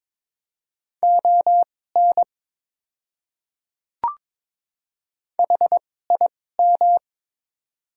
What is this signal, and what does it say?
Morse code, heard as a keyed tone, about 22 wpm, spaced as for 16 wpm, 700 Hz. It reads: ON HIM
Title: 2x Character Spacing - Top 100 Words - 22wpm